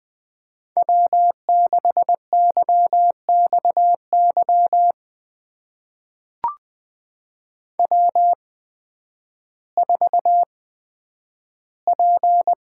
Morse code, heard as a keyed tone, about 20 wpm, spaced as for 11 wpm, 700 Hz. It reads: W6YXY W4P
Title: USA Call Signs - 5 Characters - 8x Character Spacing - 20wpm